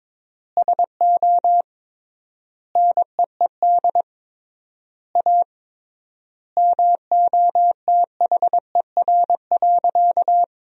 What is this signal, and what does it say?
Morse code, heard as a keyed tone, about 22 wpm, 700 Hz. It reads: SO NEED A MOTHER.
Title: 3x Word-Spacing - Sentences from Top 200 Words - 22wpm